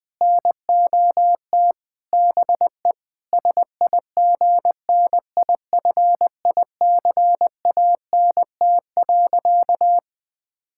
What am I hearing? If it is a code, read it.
Morse code, heard as a keyed tone, about 21 wpm, 700 Hz. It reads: NOT BE SIGNIFICANT.
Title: Sentences from Top 1000 Words - Review Part 6 - 20wpm